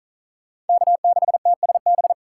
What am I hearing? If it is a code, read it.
Morse code, heard as a keyed tone, about 45 wpm, 700 Hz. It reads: K6TSB